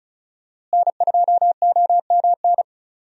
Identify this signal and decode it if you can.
Morse code, heard as a keyed tone, about 36 wpm, 700 Hz. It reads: N2OMN
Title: USA Call Signs - 5 Characters - Repeated 3x - 35wpm